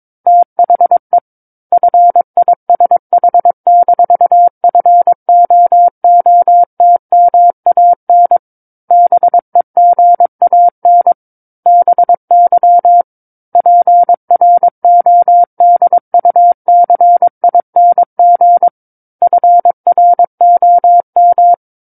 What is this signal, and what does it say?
Morse code, heard as a keyed tone, about 22 wpm, 700 Hz. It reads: THE FISH-FOOTMAN BEGAN BY PRODUCING FROM
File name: 06-Chapter-06 22 WPM